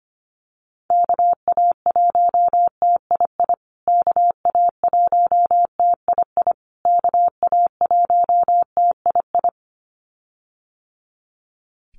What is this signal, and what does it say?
Morse code, heard as a keyed tone, about 25 wpm, 700 Hz. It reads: KA1TII KA1TII KA1TII
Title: USA Call Signs Repeated 3x 25wpm